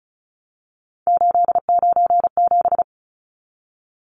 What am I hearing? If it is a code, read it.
Morse code, heard as a keyed tone, about 34 wpm, 700 Hz. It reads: 897
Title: Numbers - 3-Digit 35wpm